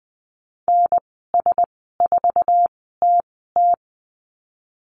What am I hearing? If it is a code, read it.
Morse code, heard as a keyed tone, about 20 wpm, spaced as for 15 wpm, 700 Hz. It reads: NS4TT
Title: 048 Lesson 9g - Number 4 - Call Signs 20wpm